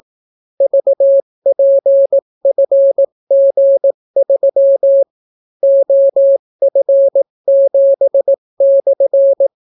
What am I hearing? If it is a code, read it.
Morse code, heard as a keyed tone, about 18 wpm, 550 Hz. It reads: VPFG3 OF7/